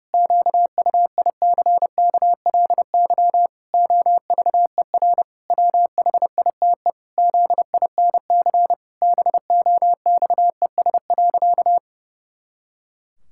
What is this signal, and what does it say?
Morse code, heard as a keyed tone, about 28 wpm, 700 Hz. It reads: QUICKLY OVER WHITE ZINC BOXES.
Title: Pangram Challenge v2 - 30wpm